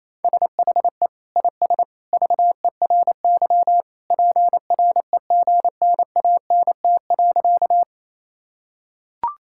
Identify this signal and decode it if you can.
Morse code, heard as a keyed tone, about 28 wpm, 700 Hz. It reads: SHE IS VERY PREGNANT.
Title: Sentences from Top 1000 Words - Review Part 6 - 28wpm